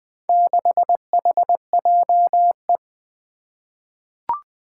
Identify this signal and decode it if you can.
Morse code, heard as a keyed tone, about 20 wpm, 700 Hz. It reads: 6HJE
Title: Four Challenging Letter-Number Combinations 20wpm